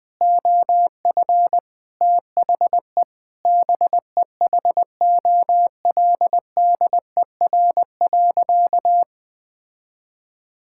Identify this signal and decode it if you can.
Morse code, heard as a keyed tone, about 20 wpm, 700 Hz. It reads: OF THE BEHOLDER.
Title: Sentences with 8-Word Phrases - 20wpm